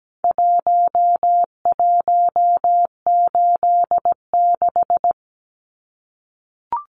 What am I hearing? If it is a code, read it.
Morse code, heard as a keyed tone, about 17 wpm, 700 Hz. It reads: 1186
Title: Numbers - Contest Style - 4-Digit 17wpm